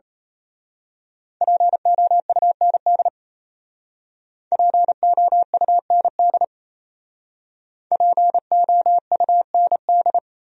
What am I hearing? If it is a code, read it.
Morse code, heard as a keyed tone, about 32 wpm, 700 Hz. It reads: POUND POUND POUND
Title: Speed-Racing Top 500 Words 28wpm